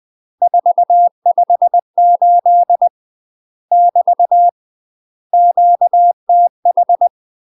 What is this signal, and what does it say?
Morse code, heard as a keyed tone, about 20 wpm, 700 Hz. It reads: 458 <BT> QTH
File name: UK MASTER QSOs TRAINING / 20WPM / UK MASTER QSOs 0049